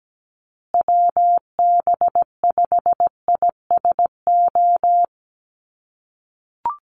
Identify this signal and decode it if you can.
Morse code, heard as a keyed tone, about 17 wpm, 700 Hz. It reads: WB5ISO
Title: USA Call Signs Repeated 3x 17wpm